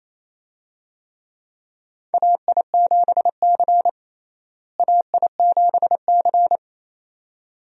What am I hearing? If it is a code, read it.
Morse code, heard as a keyed tone, about 28 wpm, 700 Hz. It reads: AI7C AI7C
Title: USA Call Signs - 4 Characters - Repeated 2x - 28wpm